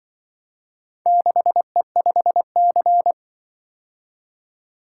Morse code, heard as a keyed tone, about 23 wpm, 700 Hz. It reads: 6E5C
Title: Speed-Racing - Four Letter-Number Combinations - 20wpm